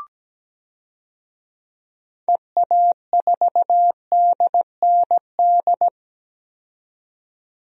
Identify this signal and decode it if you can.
Morse code, heard as a keyed tone, about 17 wpm, 700 Hz. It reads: EA4DND
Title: DXCC-100 Call Signs - 6 Characters - 17wpm